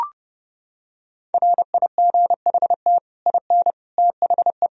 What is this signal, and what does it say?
Morse code, heard as a keyed tone, about 28 wpm, 700 Hz. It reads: RIGHT IN THE